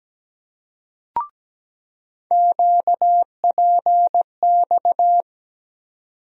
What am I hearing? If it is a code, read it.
Morse code, heard as a keyed tone, about 17 wpm, 700 Hz. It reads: QPX
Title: USA Call Sign Postfixes - 3 Letters 17wpm